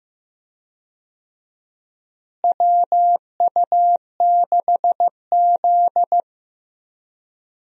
Morse code, heard as a keyed tone, about 15 wpm, 700 Hz. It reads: WU6Z
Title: USA Call Signs - 4 Characters - Repeated 2x - 15wpm